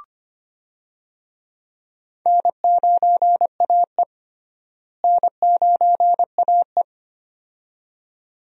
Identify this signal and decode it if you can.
Morse code, heard as a keyed tone, about 25 wpm, 700 Hz. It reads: N9AE N9AE
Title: USA Call Signs - 4 Characters - Repeated 2x - 25wpm